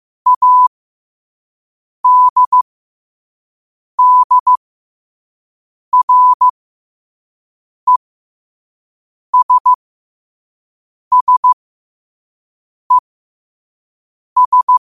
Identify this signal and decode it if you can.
Morse code, heard as a keyed tone, about 15 wpm, spaced as for 5 wpm, 1000 Hz. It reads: ADDRESSES
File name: qotdfiles / 5wpm